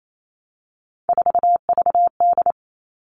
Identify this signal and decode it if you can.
Morse code, heard as a keyed tone, about 28 wpm, 700 Hz. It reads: 4VD
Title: All Cut Numbers - 3-Digit 28wpm